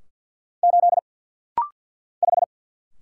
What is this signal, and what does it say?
Morse code, heard as a keyed tone, about 50 wpm, 700 Hz. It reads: 8 5
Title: Single Letter-Number Alt - Rapid-Fire 50wpm